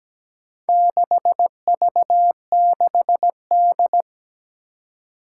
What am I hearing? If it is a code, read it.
Morse code, heard as a keyed tone, about 17 wpm, 700 Hz. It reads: 6V6D
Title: Four Challenging Letter-Number Combinations 17wpm